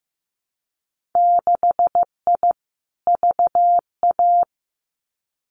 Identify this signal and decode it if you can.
Morse code, heard as a keyed tone, about 15 wpm, 700 Hz. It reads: 6I VA
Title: Winter Field Day Exchanges Encore 15wpm